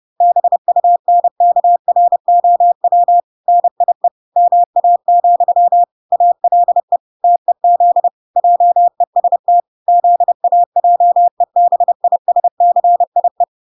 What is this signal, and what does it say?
Morse code, heard as a keyed tone, about 30 wpm, 700 Hz. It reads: BUNKROW NIE MA, ALE TEZ JEST ZAJEBISCIE